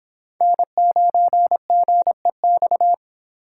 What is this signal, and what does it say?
Morse code, heard as a keyed tone, about 26 wpm, 700 Hz. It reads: N9GEX